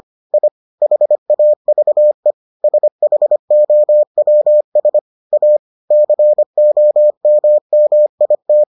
Morse code, heard as a keyed tone, about 25 wpm, 600 Hz. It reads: I HAVE SHOWS A COMMIT